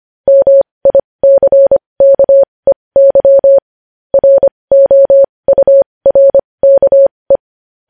Morse code, heard as a keyed tone, about 25 wpm, spaced as for 20 wpm, 550 Hz. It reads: MICKEY ROURKE
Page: news headlines in morse code at 20 wpm